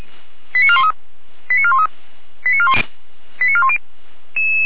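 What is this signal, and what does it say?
Five-tone selective call (ZVEI), heard as quick runs of five tones, tones of about 70 ms. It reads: 89413 89413 89410
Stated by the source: ZVEI five-tone selective call 88057 received on SDR.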